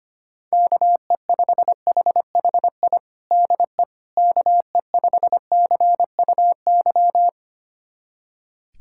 Morse code, heard as a keyed tone, about 25 wpm, 700 Hz. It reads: KE5HHI DE KE5CUY
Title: USA Callsign DE Callsign - 6 Characters - 25wpm